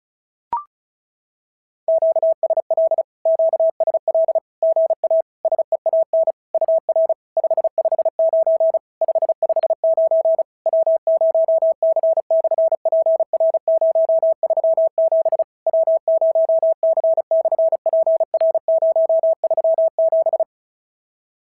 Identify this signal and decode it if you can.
Morse code, heard as a keyed tone, about 36 wpm, 650 Hz. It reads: QSL QSL GA SEAN UR 559 559 W0C/PR037 W0C/PR037
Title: SOTA QSO Practice 35wpm